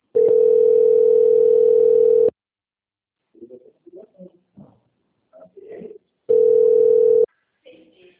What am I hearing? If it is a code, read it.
Ringback tone (North America), 440 + 480 Hz.
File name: out- - - - - .45041